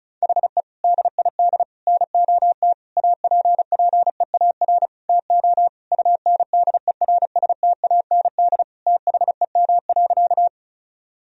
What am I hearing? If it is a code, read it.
Morse code, heard as a keyed tone, about 34 wpm, 700 Hz. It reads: HE DID NOT APPEAR TO UNDERSTAND THEM.
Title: Sentences from Top 1000 Words - Review Part 2 - 35wpm